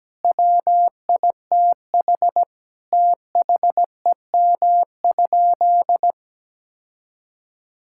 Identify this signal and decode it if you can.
Morse code, heard as a keyed tone, about 17 wpm, 700 Hz. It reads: WITH THEM?
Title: Sentences from Top 1000 Words - Review Part 6 - 17wpm